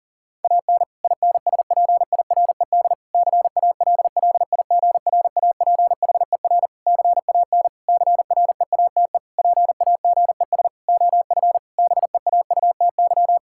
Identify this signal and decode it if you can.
Morse code, heard as a keyed tone, about 40 wpm, 700 Hz. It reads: AN INSPIRED CALLIGRAPHER CAN CREATE PAGES OF BEAUTY
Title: Pangram Challenge v2 - 40wpm